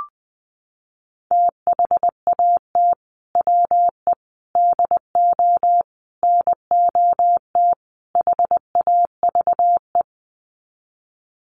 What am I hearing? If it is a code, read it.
Morse code, heard as a keyed tone, about 20 wpm, 700 Hz. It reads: THAT WE DO NOT HAVE
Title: Sets of 5 Words Encore 20wpm